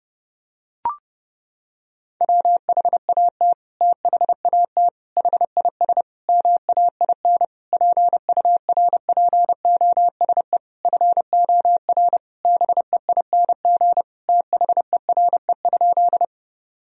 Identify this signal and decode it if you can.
Morse code, heard as a keyed tone, about 32 wpm, 700 Hz. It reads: WHAT THAT HIS MAIN PURPOSE FOR BEING THERE?
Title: Sentences from Top 1000 Words - Review Part 2 - 30wpm